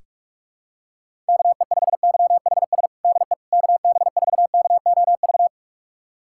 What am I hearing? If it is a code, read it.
Morse code, heard as a keyed tone, about 41 wpm, 700 Hz. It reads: KE5YHS DE KB4KOV
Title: USA Callsign DE Callsign - 6 Characters - 45wpm